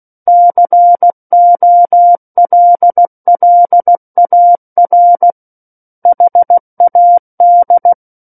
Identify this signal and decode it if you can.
Morse code, heard as a keyed tone, about 16 wpm, 700 Hz. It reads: COLLAR HAD
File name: Book One / 07-Chapter 7 - 16 WPM